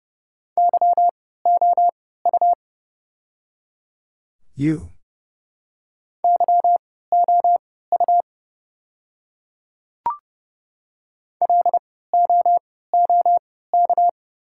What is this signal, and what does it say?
Morse code, heard as a keyed tone, about 30 wpm, spaced as for 17 wpm, 700 Hz. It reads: YOU YOU LOOK
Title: 3x Character Spacing - Top 500 Words - 30wpm